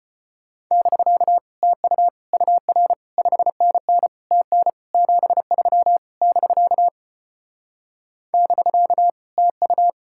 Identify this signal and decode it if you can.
Morse code, heard as a keyed tone, about 30 wpm, 700 Hz. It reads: <BK> TU UR 5NN TN 73 <BK> <BK> TU
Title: Speed-Racing Common QSO Phrases - 25wpm